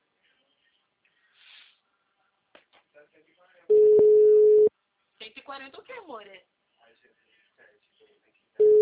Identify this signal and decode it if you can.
Ringback tone (Europe, 425 Hz), with speech.